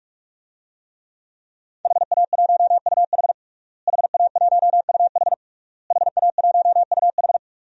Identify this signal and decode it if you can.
Morse code, heard as a keyed tone, about 41 wpm, 700 Hz. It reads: HA1UH HA1UH HA1UH
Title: DXCC-100 Call Signs - 5 Characters - Repeated 3x - 45wpm